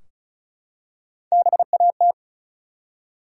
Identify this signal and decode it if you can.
Morse code, heard as a keyed tone, about 36 wpm, 700 Hz. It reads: BAT
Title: Three Letter Words - 35wpm